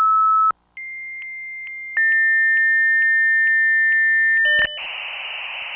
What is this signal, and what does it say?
Dial-up modem handshake.
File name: modemlab / usr